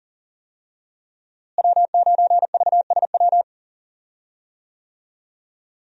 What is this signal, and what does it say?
Morse code, heard as a keyed tone, about 41 wpm, 700 Hz. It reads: W9VSW
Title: USA Call Signs - 5 Characters - Repeated 3x - 40wpm